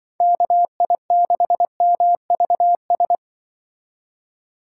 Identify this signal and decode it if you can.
Morse code, heard as a keyed tone, about 24 wpm, 700 Hz. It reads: KI6MVS
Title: Speed Racing - USA 2x3 Call Signs 20wpm